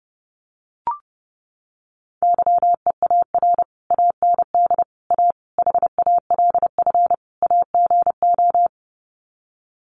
Morse code, heard as a keyed tone, about 30 wpm, 700 Hz. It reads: YEAR AND A HALF AGO
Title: Sets of 5 Words Encore 30wpm